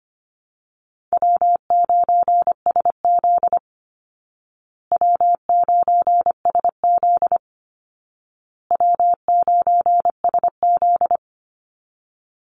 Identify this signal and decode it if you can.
Morse code, heard as a keyed tone, about 25 wpm, 700 Hz. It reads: W9SZ W9SZ W9SZ
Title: Easy Call Signs Repeated 3x - 25wpm